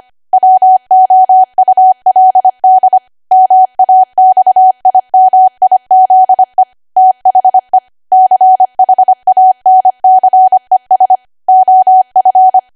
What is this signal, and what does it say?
Morse code, heard as a keyed tone, about 25 wpm, 750 Hz. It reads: WOULD MAXIMIZE THE CHANCES OF
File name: Archive / 25 WPM / 25WPM